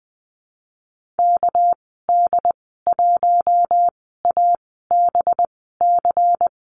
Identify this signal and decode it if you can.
Morse code, heard as a keyed tone, about 19 wpm, spaced as for 14 wpm, 700 Hz. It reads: KD1ABC
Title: 207 Lesson 33g - Letter j - Call Signs 20wpm